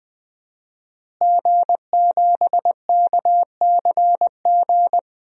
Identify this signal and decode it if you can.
Morse code, heard as a keyed tone, about 20 wpm, 700 Hz. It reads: G7KCG
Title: DXCC-100 Call Signs - 5 Characters - Repeated 3x - 20wpm